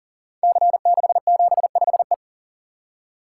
Morse code, heard as a keyed tone, about 42 wpm, 690 Hz. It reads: C675E
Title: Code Groups - 5 Letter:Numbers - 40wpm